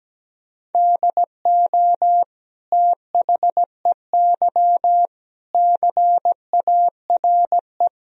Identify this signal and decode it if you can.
Morse code, heard as a keyed tone, about 17 wpm, 700 Hz. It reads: DO THEY CARE
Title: Sentences from Top 1000 Words - Review Part 3 - 17wpm